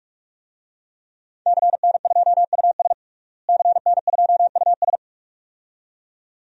Morse code, heard as a keyed tone, about 48 wpm, 700 Hz. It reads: CN2US CN2US
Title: DXCC-100 Call Signs - 5 Characters - Repeated 2x - 45wpm